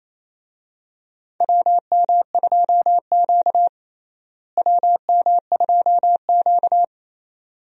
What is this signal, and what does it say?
Morse code, heard as a keyed tone, about 28 wpm, 700 Hz. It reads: WM2Q WM2Q
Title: USA Call Signs - 4 Characters - Repeated 2x - 28wpm